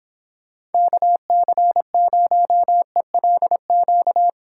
Morse code, heard as a keyed tone, about 29 wpm, 700 Hz. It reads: KC0ELQ